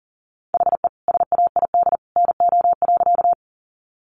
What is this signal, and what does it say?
Morse code, heard as a keyed tone, about 40 wpm, 700 Hz. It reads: HE SAID NO.
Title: Sentences from Top 100 Words 40wpm